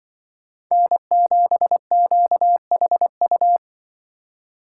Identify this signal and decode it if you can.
Morse code, heard as a keyed tone, about 24 wpm, 700 Hz. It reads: N7QHU
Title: Speed Racing - USA 1x3 Call Signs 20wpm